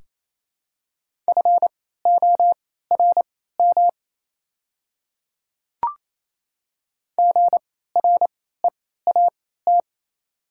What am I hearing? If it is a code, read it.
Morse code, heard as a keyed tone, about 28 wpm, spaced as for 16 wpm, 700 Hz. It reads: FORM GREAT
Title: 3x Character Spacing - Top 200 Words - 28wpm